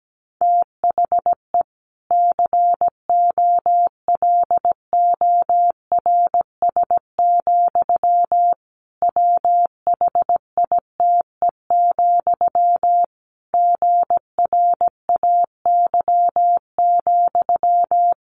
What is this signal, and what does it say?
Morse code, heard as a keyed tone, about 17 wpm, 700 Hz. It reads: THE COLORS, WHITE, GRAY,
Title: A Sound Of Thunder 17wpm